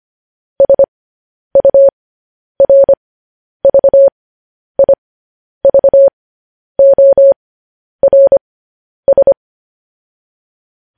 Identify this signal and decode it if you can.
Morse code, heard as a keyed tone, about 25 wpm, spaced as for 10 wpm, 550 Hz. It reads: SURVIVORS
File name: podcast / News headlines-1x-25x10